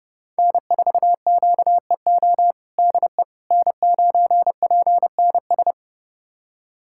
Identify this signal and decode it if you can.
Morse code, heard as a keyed tone, about 30 wpm, 700 Hz. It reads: N4QEO DE N9PNS
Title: USA Callsign DE Callsign - 5 Characters - 30wpm